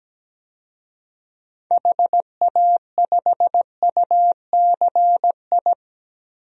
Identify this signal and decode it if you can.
Morse code, heard as a keyed tone, about 17 wpm, 700 Hz. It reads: HA5UCI
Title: DXCC-100 Call Signs - 6 Characters - 17wpm